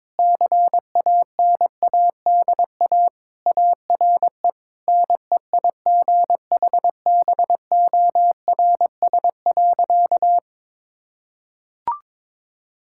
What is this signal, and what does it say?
Morse code, heard as a keyed tone, about 21 wpm, 700 Hz. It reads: CANADA ARE NEIGHBORS.
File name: practice / 22wpm / Sentences with 4-Word Phrases - 22wpm